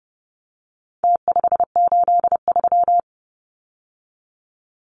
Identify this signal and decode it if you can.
Morse code, heard as a keyed tone, about 30 wpm, 700 Hz. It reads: T583
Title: Numbers - Contest Style - 4-Digit 30wpm